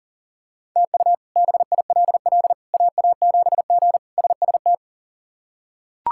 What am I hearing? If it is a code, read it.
Morse code, heard as a keyed tone, about 40 wpm, 700 Hz. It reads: TU BILL AA7G SST
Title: K1USN SST QSO Practice - 40wpm